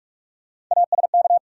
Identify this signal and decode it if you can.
Morse code, heard as a keyed tone, about 46 wpm, 700 Hz. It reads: ASK
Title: Sentences from Top 200 Words - New Word Drill - 45wpm